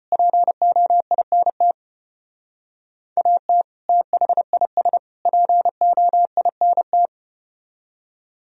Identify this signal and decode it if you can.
Morse code, heard as a keyed tone, about 34 wpm, 700 Hz. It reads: POINT AT THIS POINT